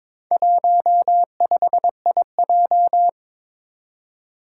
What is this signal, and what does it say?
Morse code, heard as a keyed tone, about 22 wpm, 700 Hz. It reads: 15IJ
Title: Four Challenging Letter-Number Combinations 22wpm